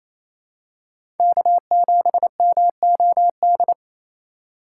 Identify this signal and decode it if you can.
Morse code, heard as a keyed tone, about 28 wpm, 700 Hz. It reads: K7MOD
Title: USA Call Signs - 3x Character Spacing - 28wpm